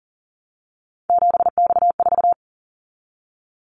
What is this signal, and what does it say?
Morse code, heard as a keyed tone, about 38 wpm, 700 Hz. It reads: 7X4
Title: Callsign Prefixes for Top 100 DXCC - 40wpm